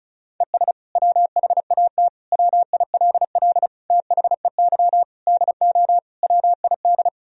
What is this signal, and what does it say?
Morse code, heard as a keyed tone, about 36 wpm, 700 Hz. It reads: ES WHAT WILL THEY DO WID
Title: Sentences from Top 200 Words and Abbreviations 35wpm